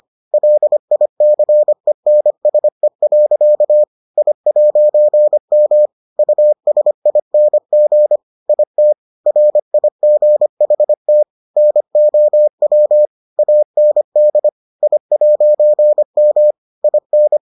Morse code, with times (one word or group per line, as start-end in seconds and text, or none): 0.3-3.8 LICENSE.
4.2-5.9 I'M
6.2-8.2 USING
8.5-8.9 IT
9.3-11.2 RIGHT
11.6-13.1 NOW
13.4-14.5 AND
14.8-16.5 I'M
16.8-17.4 IN